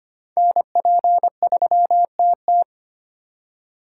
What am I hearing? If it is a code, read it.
Morse code, heard as a keyed tone, about 25 wpm, 700 Hz. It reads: NP3TT
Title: USA Call Signs - 5 Characters - 2x Character Spacing - 25wpm